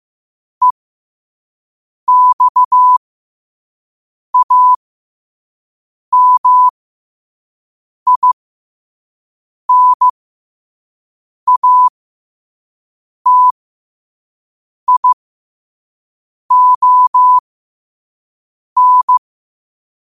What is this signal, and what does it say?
Morse code, heard as a keyed tone, about 15 wpm, spaced as for 5 wpm, 1000 Hz. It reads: EXAMINATION
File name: qotdfiles / 5wpm